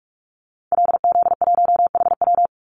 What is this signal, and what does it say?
Morse code, heard as a keyed tone, about 44 wpm, 700 Hz. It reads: LZ1HW
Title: Call Signs Repeated 2x - 45wpm